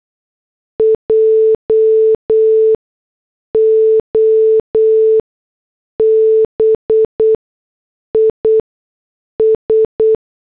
Morse code, heard as a keyed tone, about 8 wpm, spaced as for 6 wpm, 440 Hz. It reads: JOB IS
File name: morse / headlines